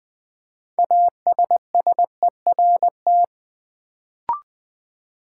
Morse code, heard as a keyed tone, about 20 wpm, 700 Hz. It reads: ASSERT
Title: Build A Word - 3 Letters At A Time - 6-Letter Words - 20wpm